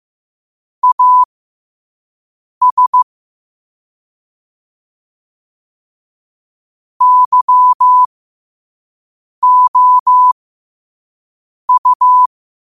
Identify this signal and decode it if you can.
Morse code, heard as a keyed tone, about 15 wpm, spaced as for 5 wpm, 1000 Hz. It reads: AS YOU
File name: qotdfiles / 5wpm